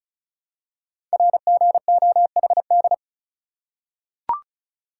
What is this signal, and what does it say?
Morse code, heard as a keyed tone, about 36 wpm, 700 Hz. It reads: RGOHD
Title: Code Groups - 5 Letters - 35wpm